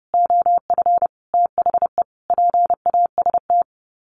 Morse code, heard as a keyed tone, about 30 wpm, 700 Hz. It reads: OF THE PAST
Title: Sets of 5 Words Encore 30wpm